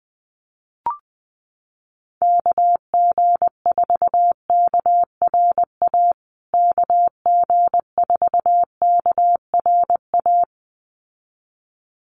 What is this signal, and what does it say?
Morse code, heard as a keyed tone, about 20 wpm, 700 Hz. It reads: KG4KRA KG4KRA
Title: USA Call Signs Repeated 2x 20wpm